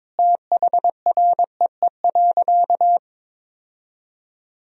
Morse code, heard as a keyed tone, about 21 wpm, 700 Hz. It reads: THREE.
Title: Sentences from Top 1000 Words - Review Part 4 - 22wpm